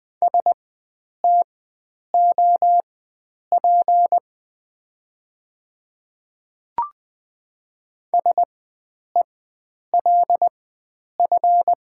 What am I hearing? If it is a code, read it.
Morse code, heard as a keyed tone, about 20 wpm, spaced as for 9 wpm, 700 Hz. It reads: STOP SELF
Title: 4x Character Spacing - Top 300 Words - 20wpm